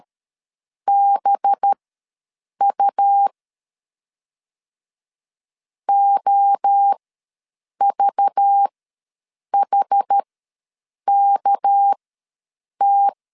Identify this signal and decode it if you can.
Morse code, heard as a keyed tone, about 13 wpm, spaced as for 7 wpm, 800 Hz. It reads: BU OVHKT
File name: LEZIONI / 7 GF9Z / 7 GF ws 12wpm 3140